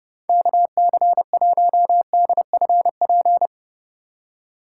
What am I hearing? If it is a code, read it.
Morse code, heard as a keyed tone, about 40 wpm, 700 Hz. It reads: KC1DFP